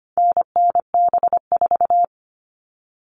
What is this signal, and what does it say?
Morse code, heard as a keyed tone, about 25 wpm, 700 Hz. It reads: NNB4
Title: All Cut Numbers - 4-Digit 25wpm